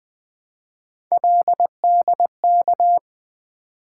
Morse code, heard as a keyed tone, about 20 wpm, 700 Hz. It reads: LDK
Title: Three Letter-Number Combinations - 20wpm